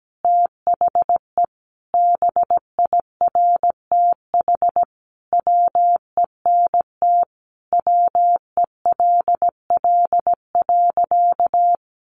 Morse code, heard as a keyed tone, about 17 wpm, 700 Hz. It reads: THE BIRTH WENT WELL.
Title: Sentences from Top 800 Words 17wpm